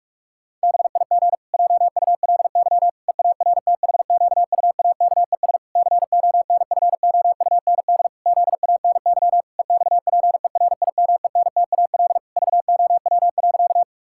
Morse code, heard as a keyed tone, about 46 wpm, 700 Hz. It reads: BIG JULY EARTHQUAKES CONFOUND ZANY EXPERIMENTAL VOW.
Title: Pangram Challenge v2 - 45wpm